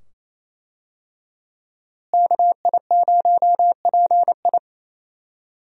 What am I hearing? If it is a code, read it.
Morse code, heard as a keyed tone, about 28 wpm, 700 Hz. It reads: KI0PI
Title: USA Call Signs - 5 Characters - Repeated 2x - 28wpm